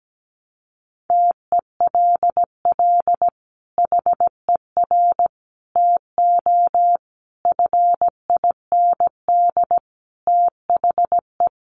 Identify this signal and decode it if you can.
Morse code, heard as a keyed tone, about 17 wpm, 700 Hz. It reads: TELL HER TO FIND THE
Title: Sentences from Top 1200 Words 17wpm